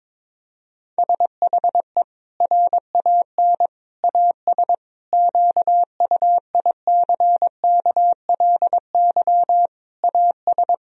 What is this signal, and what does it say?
Morse code, heard as a keyed tone, about 22 wpm, 700 Hz. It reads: SHE RAN AS QUICKLY AS
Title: Sentences from Top 1000 Words - Review Part 4 - 22wpm